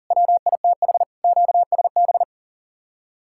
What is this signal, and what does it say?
Morse code, heard as a keyed tone, about 48 wpm, 700 Hz. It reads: WITH QSB
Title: Speed-Racing Common QSO Phrases - 40wpm